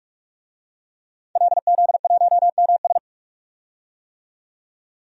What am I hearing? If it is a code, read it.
Morse code, heard as a keyed tone, about 46 wpm, 700 Hz. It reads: LZ1MS
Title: DXCC-100 Call Signs - 5 Characters - 45wpm